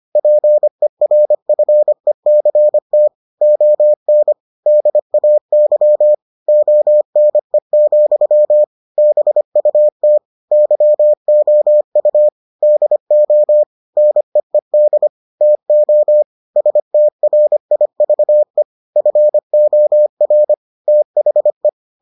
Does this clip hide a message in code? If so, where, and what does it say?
Yes: Morse code from 0.2 to 21.7 s, heard as a keyed tone, about 25 wpm, 600 Hz. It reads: PERFECT ON DAY ONE, BUT YOU DO NEED TO STRIVE FOR THE